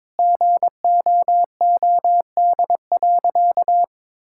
Morse code, heard as a keyed tone, about 22 wpm, 700 Hz. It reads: GOOD.
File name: practice / 22wpm / Sentences with 3-Word Phrases - 22wpm